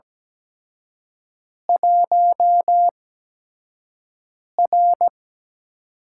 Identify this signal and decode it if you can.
Morse code, heard as a keyed tone, about 17 wpm, spaced as for 5 wpm, 700 Hz. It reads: 1R
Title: USA Call Signs - 4 Characters - 8x Character Spacing - 17wpm